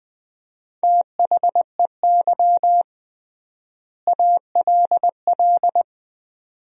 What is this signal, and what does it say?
Morse code, heard as a keyed tone, about 20 wpm, 700 Hz. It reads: THEY ALL
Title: 3x Word-Spacing - Sentences from Top 400 Words - 20wpm